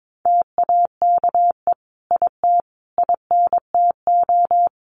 Morse code, heard as a keyed tone, about 22 wpm, 700 Hz. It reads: TAKE IT INTO
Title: Sentences from Top 1500 Words 22wpm